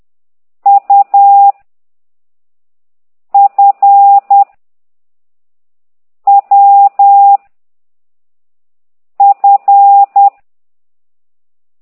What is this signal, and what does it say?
Morse code, heard as a keyed tone, about 10 wpm, 800 Hz. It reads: U F W F
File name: ct2idw / exerc9